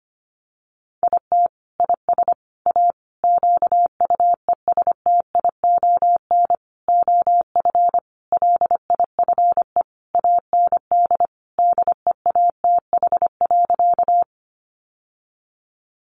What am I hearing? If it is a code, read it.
Morse code, heard as a keyed tone, about 25 wpm, 690 Hz. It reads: IT IS A QUESTION OF LIFE AND DEATH.
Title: Sentences from Top 1100 Words 25wpm